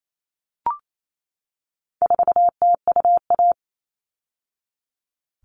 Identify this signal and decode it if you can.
Morse code, heard as a keyed tone, about 28 wpm, 700 Hz. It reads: 4TUA
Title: All Cut Numbers - 4-Digit 28wpm